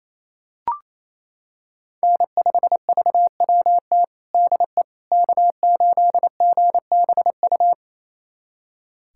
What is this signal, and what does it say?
Morse code, heard as a keyed tone, about 28 wpm, 700 Hz. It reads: N5VWT DE K8GBU
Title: USA Callsign DE Callsign - 5 Characters - 28wpm